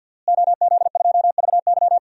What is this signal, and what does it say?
Morse code, heard as a keyed tone, about 50 wpm, 700 Hz. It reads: OZ2VY
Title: DXCC-100 Call Signs - 5 Characters - Repeated 3x - 50wpm